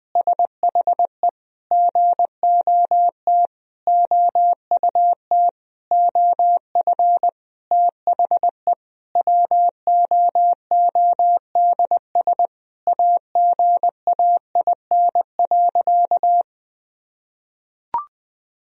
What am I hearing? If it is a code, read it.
Morse code, heard as a keyed tone, about 20 wpm, 700 Hz. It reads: SHE GOT OUT OF THE WOODS AGAIN.
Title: Sentences from Top 1000 Words - Review Part 2 - 20wpm